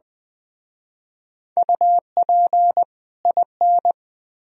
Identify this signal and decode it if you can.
Morse code, heard as a keyed tone, about 20 wpm, 700 Hz. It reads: UP IN
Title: Speed-Racing - Sets of 2 Words 17wpm